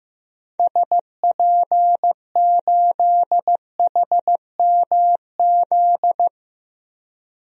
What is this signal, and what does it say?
Morse code, heard as a keyed tone, about 15 wpm, 700 Hz. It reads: SP8HMZ
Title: DXCC-100 Call Signs - 6 Characters - Repeated 3x - 15wpm